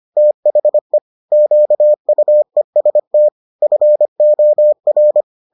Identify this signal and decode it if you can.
Morse code, heard as a keyed tone, about 25 wpm, 600 Hz. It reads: THE QUEST FOR